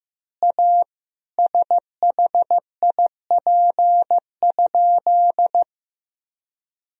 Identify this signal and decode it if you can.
Morse code, heard as a keyed tone, about 15 wpm, 700 Hz. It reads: A SHIP?
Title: Sentences from Top 1000 Words - Review Part 2 - 15wpm